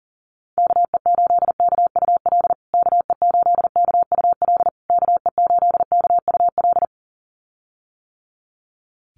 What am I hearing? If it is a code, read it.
Morse code, heard as a keyed tone, about 40 wpm, 700 Hz. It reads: KE8KUL KE8KUL KE8KUL
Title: USA Call Signs Repeated 3x 40wpm